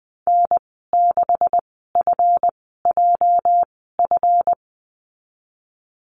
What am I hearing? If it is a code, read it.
Morse code, heard as a keyed tone, about 20 wpm, spaced as for 15 wpm, 700 Hz. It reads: N6FJF
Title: 231 Lesson 37g - Letter x - Call Signs 20wpm